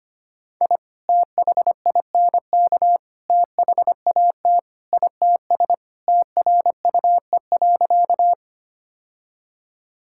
Morse code, heard as a keyed tone, about 25 wpm, 700 Hz. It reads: I THINK THAT ITS TRUE.
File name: practice / 25wpm / Sentences with 2-Word Phrases - 25wpm